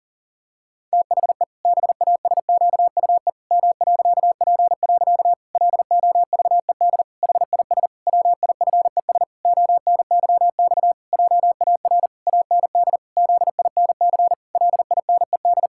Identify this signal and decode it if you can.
Morse code, heard as a keyed tone, about 40 wpm, 700 Hz. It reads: THE BASQUE M.P. LOVED HIS WIFES ONYX JAR AND ZINC LINED
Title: Pangram Challenge v2 - 40wpm